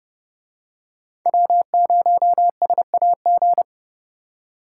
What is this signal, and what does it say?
Morse code, heard as a keyed tone, about 29 wpm, 700 Hz. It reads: W0SAG